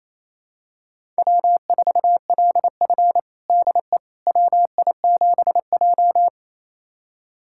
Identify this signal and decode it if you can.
Morse code, heard as a keyed tone, about 28 wpm, 700 Hz. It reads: W4LF DE WI7J
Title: USA Callsign DE Callsign - 4 Characters - 28wpm